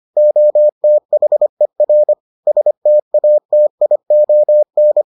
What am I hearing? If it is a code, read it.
Morse code, heard as a keyed tone, about 25 wpm, 600 Hz. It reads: OTHER STATION